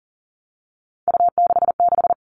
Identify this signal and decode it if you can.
Morse code, heard as a keyed tone, about 40 wpm, 700 Hz. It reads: U66
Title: All Cut Numbers - 3-Digit 40wpm